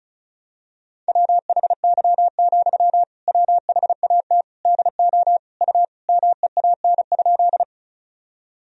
Morse code, heard as a keyed tone, about 36 wpm, 700 Hz. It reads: WHY, WHAT DO U MEAN?
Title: Sentences from Top 200 Words and Abbreviations 35wpm